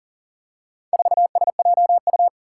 Morse code, heard as a keyed tone, about 40 wpm, 700 Hz. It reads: 4SJU
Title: Four Challenging Letter-Number Combinations 40wpm